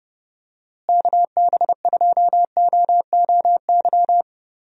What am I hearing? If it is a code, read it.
Morse code, heard as a keyed tone, about 30 wpm, 700 Hz. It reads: KB2OOY
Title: USA Call Signs - 6 Characters - Repeated 2x - 30wpm